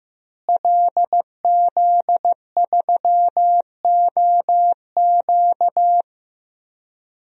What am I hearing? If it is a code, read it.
Morse code, heard as a keyed tone, about 16 wpm, 700 Hz. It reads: LZ3OQ
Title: DXCC-100 Call Signs - 5 Characters - Repeated 3x - 15wpm